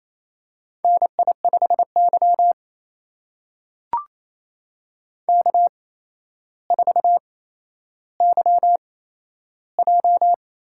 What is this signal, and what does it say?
Morse code, heard as a keyed tone, about 28 wpm, spaced as for 12 wpm, 700 Hz. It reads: NI5Y K4YJ
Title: USA Call Signs - 4 Characters - 8x Character Spacing - 28wpm